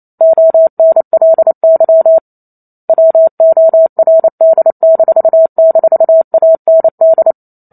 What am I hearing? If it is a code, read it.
Morse code, heard as a keyed tone, about 28 wpm, 650 Hz. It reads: ONLY WORD--AND